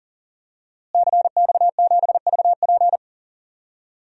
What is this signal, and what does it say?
Morse code, heard as a keyed tone, about 40 wpm, 700 Hz. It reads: CX7VP
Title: DXCC-100 Call Signs - 5 Characters - Repeated 2x - 40wpm